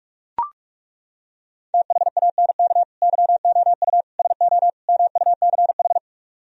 Morse code, heard as a keyed tone, about 38 wpm, 700 Hz. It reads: THANK YOU SO MUCH